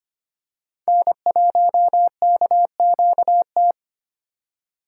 Morse code, heard as a keyed tone, about 25 wpm, 700 Hz. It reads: N1KQT
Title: USA Call Signs - 4x Character Spacing - 25wpm